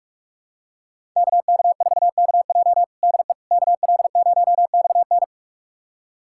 Morse code, heard as a keyed tone, about 46 wpm, 700 Hz. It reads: KK4KJ DE KL0XN
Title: USA Callsign DE Callsign - 5 Characters - 45wpm